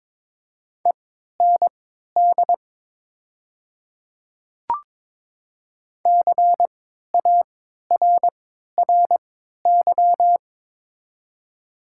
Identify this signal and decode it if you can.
Morse code, heard as a keyed tone, about 22 wpm, spaced as for 13 wpm, 700 Hz. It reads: END CARRY
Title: 3x Character Spacing - Top 300 Words - 22wpm